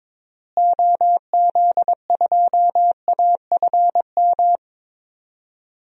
Morse code, heard as a keyed tone, about 22 wpm, 700 Hz. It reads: OZ2AFM
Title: DXCC-100 Call Signs - 6 Characters - Repeated 2x - 22wpm